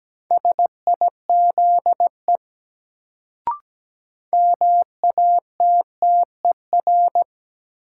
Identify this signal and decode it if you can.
Morse code, heard as a keyed tone, about 17 wpm, 700 Hz. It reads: SIZE MATTER
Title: Sets of 3 Words - New Word Drill - 17wpm